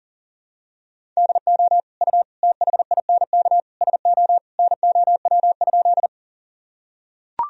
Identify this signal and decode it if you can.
Morse code, heard as a keyed tone, about 40 wpm, 700 Hz. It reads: DO U THINK SO NOW?
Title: Sentences from Top 200 Words and Abbreviations 40wpm